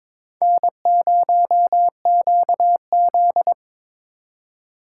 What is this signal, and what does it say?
Morse code, heard as a keyed tone, about 22 wpm, 700 Hz. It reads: N0QZ
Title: USA Call Signs - 4 Characters - 2x Character Spacing - 22wpm